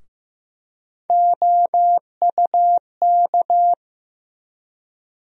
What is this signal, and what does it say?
Morse code, heard as a keyed tone, about 15 wpm, 700 Hz. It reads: OUK
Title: USA Call Sign Postfixes - 3 Letters 15wpm